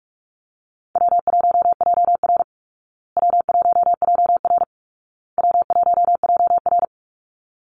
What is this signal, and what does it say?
Morse code, heard as a keyed tone, about 44 wpm, 700 Hz. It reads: W1JR W1JR W1JR
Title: Easy Call Signs Repeated 3x - 45wpm